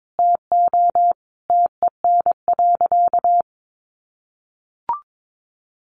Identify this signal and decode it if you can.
Morse code, heard as a keyed tone, about 22 wpm, 700 Hz. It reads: TO TEN.
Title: Easy Phrases 22wpm